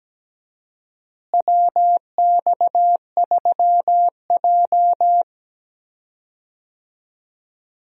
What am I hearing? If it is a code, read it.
Morse code, heard as a keyed tone, about 17 wpm, 700 Hz. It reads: WX3J
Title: USA Call Signs - 4 Characters - Repeated 2x - 17wpm